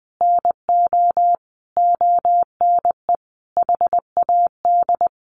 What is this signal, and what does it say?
Morse code, heard as a keyed tone, about 19 wpm, 700 Hz. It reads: NO ONE HAD
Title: Sentences from Top 1200 Words 20wpm